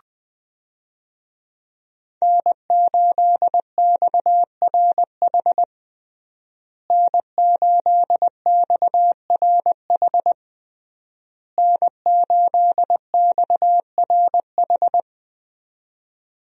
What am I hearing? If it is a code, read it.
Morse code, heard as a keyed tone, about 20 wpm, 700 Hz. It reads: N8XRH N8XRH N8XRH
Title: USA Call Signs - 5 Characters - Repeated 3x - 20wpm